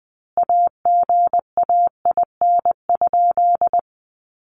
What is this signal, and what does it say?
Morse code, heard as a keyed tone, about 20 wpm, 700 Hz. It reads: AGAIN?
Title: Sentences from Top 1200 Words 20wpm